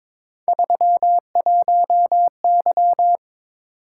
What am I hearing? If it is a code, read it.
Morse code, heard as a keyed tone, about 22 wpm, 700 Hz. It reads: 31Y
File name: Three Letter-Number Combinations - 22wpm